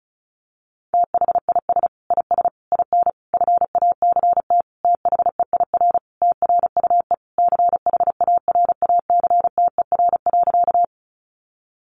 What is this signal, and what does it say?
Morse code, heard as a keyed tone, about 34 wpm, 700 Hz. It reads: THIS IS IN FACT THEIR TRUE CHARACTER.
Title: Sentences from Top 1200 Words 35wpm